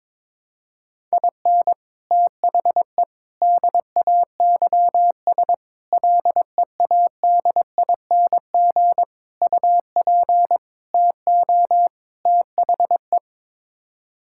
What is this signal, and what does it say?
Morse code, heard as a keyed tone, about 22 wpm, 700 Hz. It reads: IN THE DAYS LEADING UP TO THE
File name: practice / 22wpm / Sets of 7 Words - 22wpm